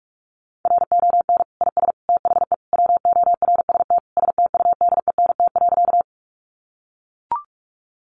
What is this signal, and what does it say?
Morse code, heard as a keyed tone, about 44 wpm, 700 Hz. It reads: RON IS THE WORST STUDENT.
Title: Additional Easy Phrases 45wpm